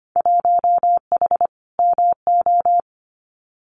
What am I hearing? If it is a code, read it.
Morse code, heard as a keyed tone, about 25 wpm, 700 Hz. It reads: 1H MO
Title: Winter Field Day Exchanges 25wpm